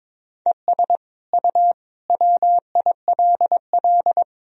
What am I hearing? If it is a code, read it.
Morse code, heard as a keyed tone, about 22 wpm, 700 Hz. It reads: ES U WILL
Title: Sentences from Top 400 Words and Abbreviations 22wpm